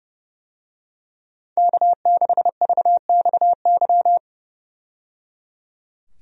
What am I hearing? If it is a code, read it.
Morse code, heard as a keyed tone, about 30 wpm, 700 Hz. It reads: K6VXY
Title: USA Call Signs - 5 Characters - 30wpm